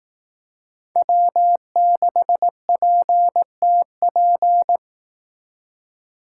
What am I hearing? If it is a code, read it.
Morse code, heard as a keyed tone, about 18 wpm, 700 Hz. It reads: W6PTP
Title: Speed Racing - USA 1x3 Call Signs 15wpm